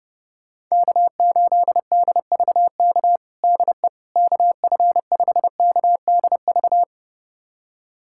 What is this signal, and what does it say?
Morse code, heard as a keyed tone, about 32 wpm, 700 Hz. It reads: K8DVK DE KF5KDV
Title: USA Callsign DE Callsign 30wpm